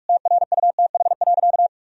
Morse code, heard as a keyed tone, about 46 wpm, 700 Hz. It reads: TRUTH.
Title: Sentences from Top 1000 Words - Review Part 2 - 45wpm